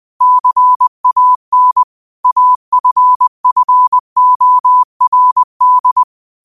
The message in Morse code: CAN AFFORD

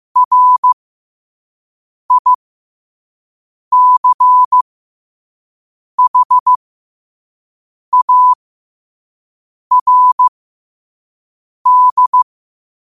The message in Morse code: RICHARD

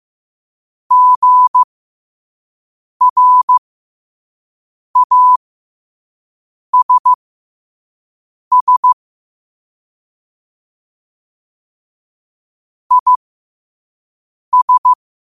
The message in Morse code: GRASS IS

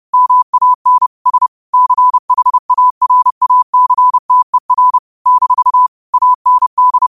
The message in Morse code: MAN S CHARACTER - AND

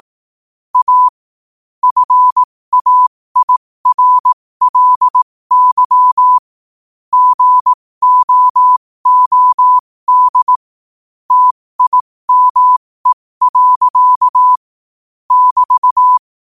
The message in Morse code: A FAIRLY GOOD TIME. <BT>

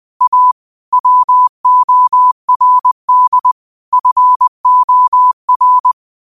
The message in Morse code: A WORD FOR

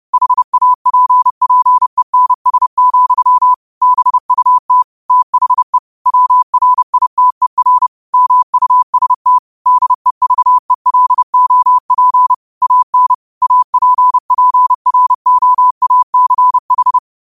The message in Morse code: HAPPENS, BUT THE WRITER MUST DEVELOP AN APPROACH

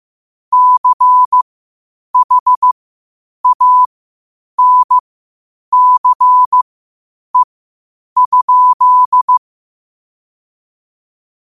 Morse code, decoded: CHANCE?

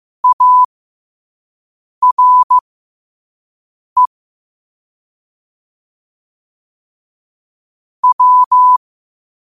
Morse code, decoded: ARE W

